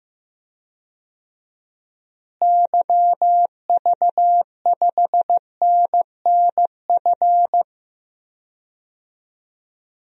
Morse code, heard as a keyed tone, about 15 wpm, 700 Hz. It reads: YV5NNF
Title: DXCC-100 Call Signs - 6 Characters - Repeated 3x - 15wpm